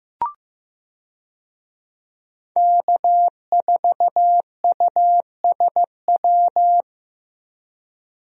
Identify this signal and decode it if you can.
Morse code, heard as a keyed tone, about 15 wpm, 700 Hz. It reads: K4USW
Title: USA Call Signs - 5 Characters - Repeated 3x - 15wpm